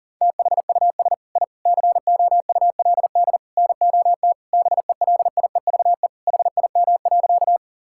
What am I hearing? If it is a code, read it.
Morse code, heard as a keyed tone, about 40 wpm, 700 Hz. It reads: THUS I COULD NOT BELIEVE HIM.
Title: Sentences from Top 1000 Words - Review Part 5 - 40wpm